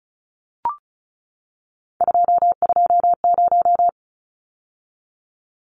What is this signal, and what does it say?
Morse code, heard as a keyed tone, about 34 wpm, 700 Hz. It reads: 220
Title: Numbers - 3-Digit 35wpm